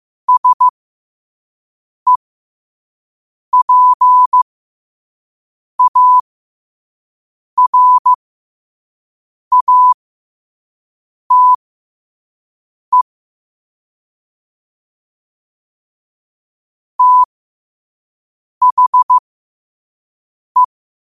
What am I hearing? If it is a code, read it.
Morse code, heard as a keyed tone, about 15 wpm, spaced as for 5 wpm, 1000 Hz. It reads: SEPARATE THE
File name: qotdfiles / 5wpm